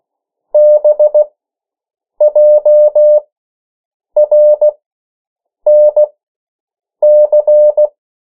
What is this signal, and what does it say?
Morse code, heard as a keyed tone, about 16 wpm, spaced as for 7 wpm, 600 Hz. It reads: BJRNC
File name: morse mp3 / 14 urok 13